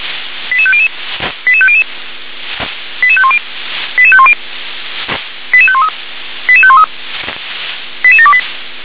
Five-tone selective call (ZVEI), heard as quick runs of five tones, tones of about 70 ms. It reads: 88500 88500 88510 88510 88513 88513 88728
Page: SDR/ZVEI